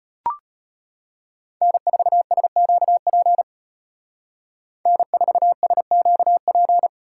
Morse code, heard as a keyed tone, about 30 wpm, 700 Hz. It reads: N4SQP N4SQP
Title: Speed Racing - USA 1x3 Call Signs 25wpm